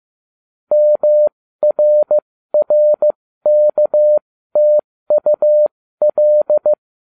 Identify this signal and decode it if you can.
Morse code, heard as a keyed tone, about 15 wpm, spaced as for 13 wpm, 600 Hz. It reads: MRRKTUL
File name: cw / ubacw08 2